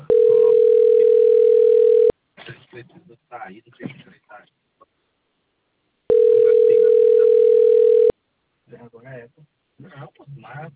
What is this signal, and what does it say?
Ringback tone (North America), 440 + 480 Hz, with speech.
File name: out- - - - - .42565